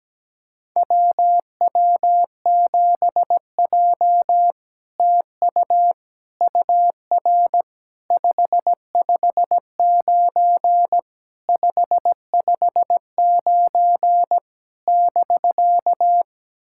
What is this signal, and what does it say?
Morse code, heard as a keyed tone, about 17 wpm, 700 Hz. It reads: WW7J TU UR 559 559 <BK>
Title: SOTA QSO Practice 17wpm